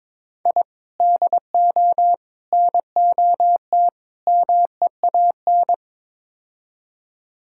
Morse code, heard as a keyed tone, about 22 wpm, 700 Hz. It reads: I DO NOT MEAN
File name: practice / 22wpm / Build A Phrase - 5-Word Phrases - 22wpm